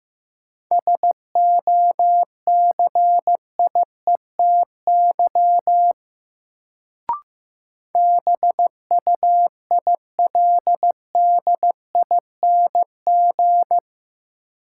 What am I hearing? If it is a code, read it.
Morse code, heard as a keyed tone, about 15 wpm, 700 Hz. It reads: SOCIETY BUILDING
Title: Sentences from Top 700 Words - New Word Drill - 15wpm